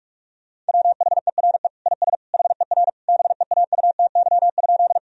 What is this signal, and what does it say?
Morse code, heard as a keyed tone, about 46 wpm, 700 Hz. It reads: WHERE IS HER BEAUTY?
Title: Sentences from Top 1000 Words - Review Part 2 - 45wpm